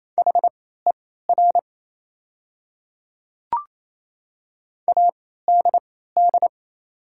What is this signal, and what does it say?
Morse code, heard as a keyed tone, about 28 wpm, spaced as for 16 wpm, 700 Hz. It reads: HER ADD
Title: 3x Character Spacing - Top 200 Words - 28wpm